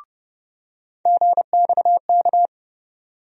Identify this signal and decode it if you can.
Morse code, heard as a keyed tone, about 30 wpm, 700 Hz. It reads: GXK
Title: USA Call Sign Postfixes - 3 Letters 30wpm